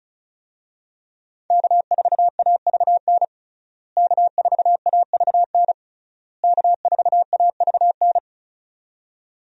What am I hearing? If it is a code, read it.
Morse code, heard as a keyed tone, about 36 wpm, 700 Hz. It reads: K4AVN K4AVN K4AVN
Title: USA Call Signs - 5 Characters - Repeated 3x - 35wpm